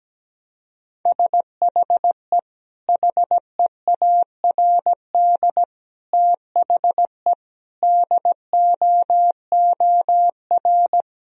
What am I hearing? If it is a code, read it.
Morse code, heard as a keyed tone, about 17 wpm, 700 Hz. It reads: SHE HEARD THE DOOR
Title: Sentences from Top 1000 Words - Review Part 1 - 17wpm